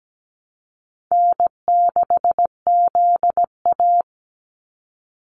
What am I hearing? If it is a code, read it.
Morse code, heard as a keyed tone, about 17 wpm, 700 Hz. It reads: N6ZA
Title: NA SOTA Activator Callsigns 17wpm